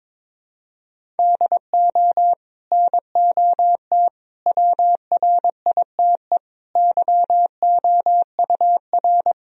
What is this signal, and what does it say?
Morse code, heard as a keyed tone, about 22 wpm, 700 Hz. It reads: DO NOT WRITE YOUR
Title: Sentences from Top 1000 Words - Review Part 1 - 22wpm